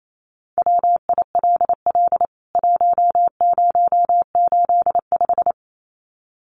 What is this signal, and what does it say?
Morse code, heard as a keyed tone, about 28 wpm, 700 Hz. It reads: WILL 1085
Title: CWT Exchange 28wpm